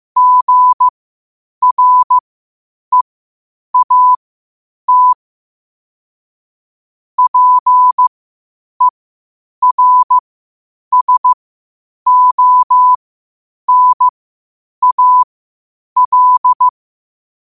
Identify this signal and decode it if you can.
Morse code, heard as a keyed tone, about 15 wpm, spaced as for 8 wpm, 1000 Hz. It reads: GREAT PERSONAL
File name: qotdfiles / 8wpm